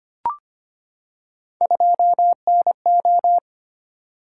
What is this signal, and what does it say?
Morse code, heard as a keyed tone, about 25 wpm, 700 Hz. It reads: 2NO
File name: Three Letter-Number Combinations - 25wpm